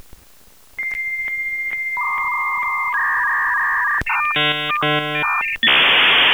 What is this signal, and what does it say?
Dial-up modem handshake.